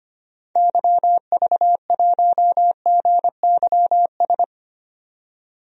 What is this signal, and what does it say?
Morse code, heard as a keyed tone, about 25 wpm, 700 Hz. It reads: YV1GYS
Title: DXCC-100 Call Signs - 6 Characters - 25wpm